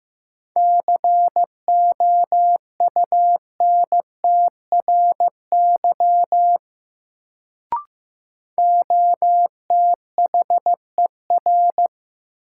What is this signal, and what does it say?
Morse code, heard as a keyed tone, about 15 wpm, 700 Hz. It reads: COUNTRY OTHER
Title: Sets of 2 Words - New Word Drill - 15wpm